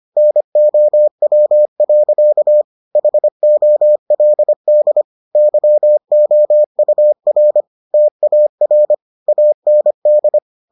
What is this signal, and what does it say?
Morse code, heard as a keyed tone, about 25 wpm, 600 Hz. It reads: NOW. HOLD YOUR TAR AND